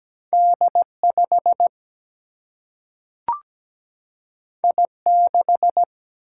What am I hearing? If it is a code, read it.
Morse code, heard as a keyed tone, about 17 wpm, 700 Hz. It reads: D5 I6
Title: Single-Letter Single-Number 17wpm